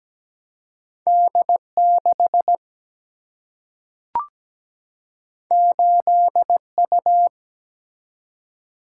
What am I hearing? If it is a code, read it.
Morse code, heard as a keyed tone, about 17 wpm, 700 Hz. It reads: D 6 8 U
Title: Single-Letter Single-Number 17wpm